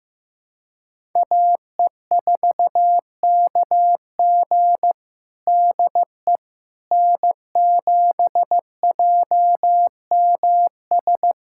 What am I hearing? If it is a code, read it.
Morse code, heard as a keyed tone, about 15 wpm, 700 Hz. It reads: AE4KG DE N7JMS
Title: USA Callsign DE Callsign - 5 Characters - 15wpm